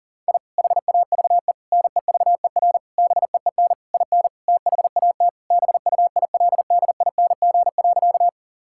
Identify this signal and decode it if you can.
Morse code, heard as a keyed tone, about 40 wpm, 700 Hz. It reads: I HAVE NEVER BEEN IN THAT BUILDING.
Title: Sentences from Top 1000 Words - Review Part 5 - 40wpm